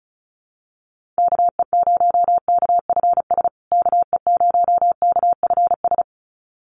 Morse code, heard as a keyed tone, about 34 wpm, 700 Hz. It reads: KE0KFS KE0KFS
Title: USA Call Signs Repeated 2x 35wpm